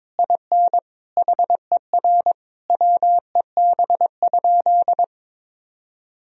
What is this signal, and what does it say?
Morse code, heard as a keyed tone, about 22 wpm, 700 Hz. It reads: IN HER WEB?
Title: Sentences from Top 1000 Words - Review Part 4 - 22wpm